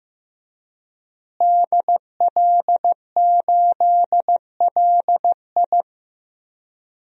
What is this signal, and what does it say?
Morse code, heard as a keyed tone, about 15 wpm, 700 Hz. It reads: DL8LI
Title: DXCC-100 Call Signs - 5 Characters - Repeated 3x - 15wpm